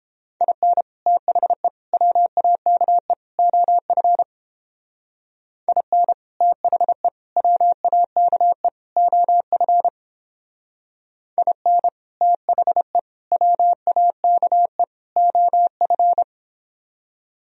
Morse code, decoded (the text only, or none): IN THE WAKE OF IN THE WAKE OF IN THE WAKE OF